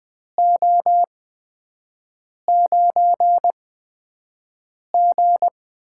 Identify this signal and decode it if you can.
Morse code, heard as a keyed tone, about 20 wpm, spaced as for 5 wpm, 700 Hz. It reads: O9G
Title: USA Call Signs - 4 Characters - 8x Character Spacing - 20wpm